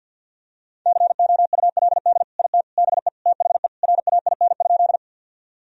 Morse code, heard as a keyed tone, about 52 wpm, 700 Hz. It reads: COULD IT BE THE RAIN?
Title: Sentences from Top 1000 Words - Review Part 1 - 50wpm